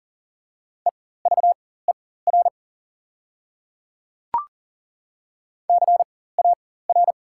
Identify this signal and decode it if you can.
Morse code, heard as a keyed tone, about 40 wpm, spaced as for 19 wpm, 690 Hz. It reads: EVER CAR
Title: 4x Character Spacing - Top 500 Words - 40wpm